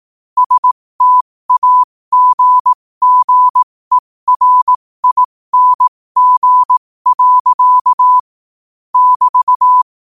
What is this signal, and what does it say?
Morse code, heard as a keyed tone, about 18 wpm, spaced as for 15 wpm, 1000 Hz. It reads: STAGGERING. <BT>
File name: qotdfiles / 15wpm